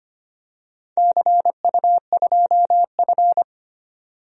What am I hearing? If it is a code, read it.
Morse code, heard as a keyed tone, about 25 wpm, 700 Hz. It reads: CU2F
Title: DXCC-100 Call Signs - 4 Characters - 25wpm